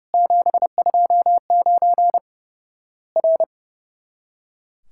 Morse code, heard as a keyed tone, about 30 wpm, 700 Hz. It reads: 729 R
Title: Worked All Europe DX Contest - QTC Exchange - 30wpm